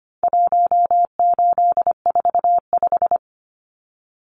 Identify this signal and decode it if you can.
Morse code, heard as a keyed tone, about 25 wpm, 700 Hz. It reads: 1845
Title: Numbers - 4-Digit 25wpm